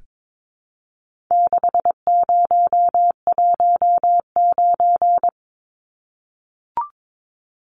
Morse code, heard as a keyed tone, about 22 wpm, 700 Hz. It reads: 6019
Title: Numbers - 4-Digit 22wpm